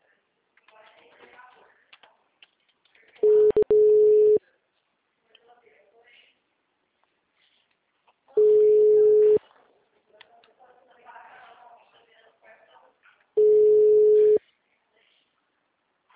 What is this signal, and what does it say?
Ringback tone (Europe, 425 Hz), with speech.